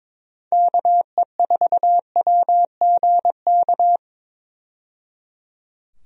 Morse code, heard as a keyed tone, about 22 wpm, 700 Hz. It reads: KE4WGK
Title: USA Call Signs - 6 Characters - 22wpm